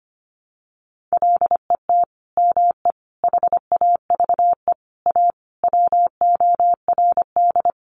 Morse code, heard as a keyed tone, about 25 wpm, 700 Hz. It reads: LET ME HAVE A WORD
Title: Sentences from Top 900 Words 25wpm